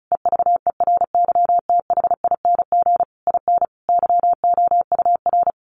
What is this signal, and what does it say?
Morse code, heard as a keyed tone, about 34 wpm, 700 Hz. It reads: EVERYTHING IN YOUR
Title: Sentences from Top 700 Words 35wpm